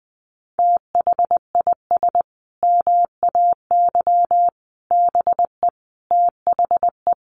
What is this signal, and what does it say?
Morse code, heard as a keyed tone, about 19 wpm, 700 Hz. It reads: THIS MAY BE THE
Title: Sentences from Top 900 Words 20wpm